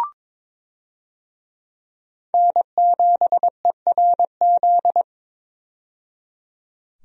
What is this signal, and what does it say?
Morse code, heard as a keyed tone, about 22 wpm, 700 Hz. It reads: N7ERZ
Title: USA Call Signs - 5 Characters - 22wpm